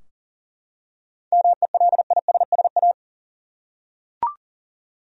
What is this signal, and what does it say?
Morse code, heard as a keyed tone, about 40 wpm, 700 Hz. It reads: MELISSA
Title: USA Common Names - 40wpm